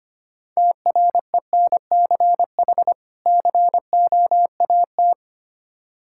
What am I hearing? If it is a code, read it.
Morse code, heard as a keyed tone, about 25 wpm, 700 Hz. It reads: TRENCH COAT
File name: practice / 25wpm / Difficult Words with Hints - 25wpm